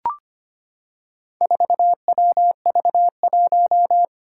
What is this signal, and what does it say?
Morse code, heard as a keyed tone, about 25 wpm, 700 Hz. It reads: 4WV1
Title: Four Challenging Letter-Number Combinations 25wpm